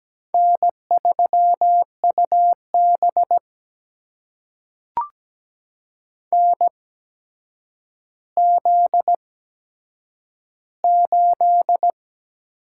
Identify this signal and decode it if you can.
Morse code, heard as a keyed tone, about 17 wpm, spaced as for 8 wpm, 700 Hz. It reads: N3UB NZ8
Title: USA Call Signs - 4 Characters - 8x Character Spacing - 17wpm